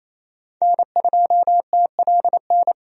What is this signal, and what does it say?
Morse code, heard as a keyed tone, about 28 wpm, 700 Hz. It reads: N2TLN
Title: USA Call Signs - 5 Characters - 28wpm